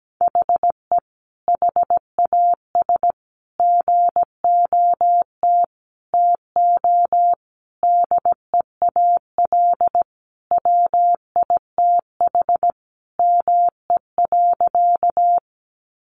Morse code, heard as a keyed tone, about 17 wpm, 700 Hz. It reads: HE HAS GOT TO DEAL WITH ME.
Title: Sentences from Top 800 Words 17wpm